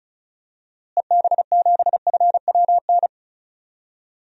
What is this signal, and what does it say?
Morse code, heard as a keyed tone, about 36 wpm, 700 Hz. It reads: EB7FWN
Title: DXCC-100 Call Signs - 6 Characters - Repeated 2x - 35wpm